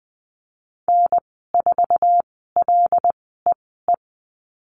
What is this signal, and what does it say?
Morse code, heard as a keyed tone, about 20 wpm, spaced as for 15 wpm, 700 Hz. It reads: N4LEE
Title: 088 Lesson 15g - Number 5 - Call Signs 20wpm